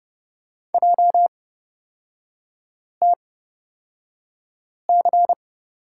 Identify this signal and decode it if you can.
Morse code, heard as a keyed tone, about 30 wpm, spaced as for 5 wpm, 700 Hz. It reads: JTC
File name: Lesson 37e - Letter x - Three Characters 30wpm